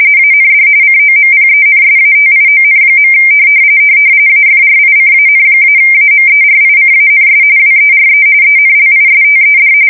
RTTY, about 45.45 baud. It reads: AND COMPENSATION. AFRICAN-AMERICANS WERE ENLISTED
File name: fsk170 / usnavy4